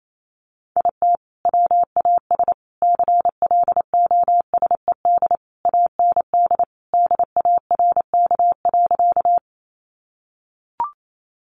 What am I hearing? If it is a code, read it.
Morse code, heard as a keyed tone, about 28 wpm, 700 Hz. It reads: IT WAS CLOSED AND DARK.
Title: Sentences from Top 1400 Words 28wpm